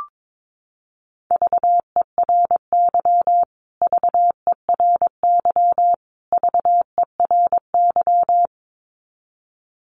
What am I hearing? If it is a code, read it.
Morse code, heard as a keyed tone, about 22 wpm, 700 Hz. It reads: VERY VERY VERY
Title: Top 500 Words Repeated 22wpm